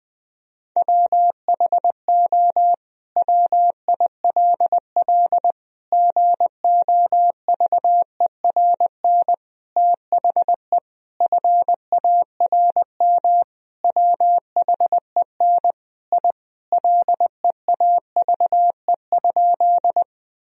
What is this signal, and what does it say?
Morse code, heard as a keyed tone, about 21 wpm, 700 Hz. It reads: WHO WILL GOVERN THE FARM WHEN I LEAVE?
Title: Sentences from Top 1000 Words - Review Part 2 - 20wpm